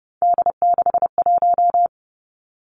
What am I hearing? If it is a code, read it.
Morse code, heard as a keyed tone, about 32 wpm, 700 Hz. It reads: D61
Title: Three Challenging Letter-Number Combinations 30wpm